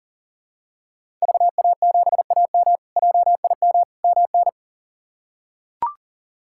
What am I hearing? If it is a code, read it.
Morse code, heard as a keyed tone, about 40 wpm, 700 Hz. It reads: VA7AM JIM MN
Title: K1USN SST QSO Practice - 40wpm